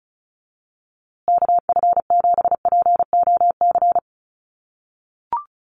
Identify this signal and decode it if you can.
Morse code, heard as a keyed tone, about 34 wpm, 700 Hz. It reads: KF7POC
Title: USA Call Signs Repeated 2x 35wpm